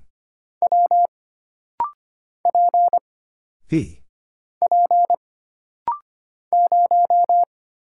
Morse code, heard as a keyed tone, about 25 wpm, 700 Hz. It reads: W P P 0
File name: Single Character (Letter:Number:Punctuation:Prosign) - Rapid-Fire - 25wpm